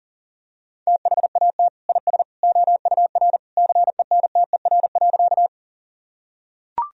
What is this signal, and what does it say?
Morse code, heard as a keyed tone, about 40 wpm, 700 Hz. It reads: THAT IS OUR CENTER.
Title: Sentences from Top 1000 Words - Review Part 2 - 40wpm